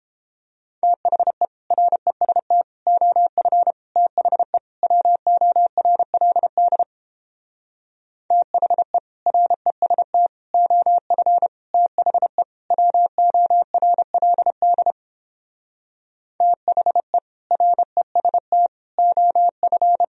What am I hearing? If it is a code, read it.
Morse code, heard as a keyed tone, about 31 wpm, 700 Hz. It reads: THE REST OF THE WORLD THE REST OF THE WORLD THE REST OF